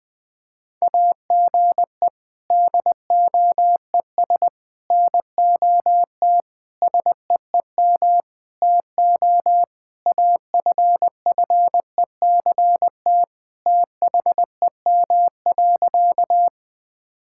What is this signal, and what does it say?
Morse code, heard as a keyed tone, about 20 wpm, 700 Hz. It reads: AGE DOES NOT SEEM TO AFFECT THEM.
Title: Sentences from Top 1000 Words - Review Part 6 - 20wpm